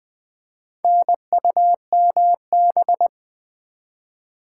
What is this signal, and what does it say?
Morse code, heard as a keyed tone, about 20 wpm, 690 Hz. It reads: NUMB
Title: Build A Word - 2 Letters At A Time - 6-Letter Words - 20wpm